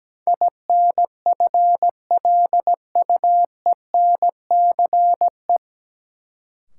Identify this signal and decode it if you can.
Morse code, heard as a keyed tone, about 17 wpm, 700 Hz. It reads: INFLUENCE
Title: Sentences from Top 1100 Words - New Word Drill Alt - 17wpm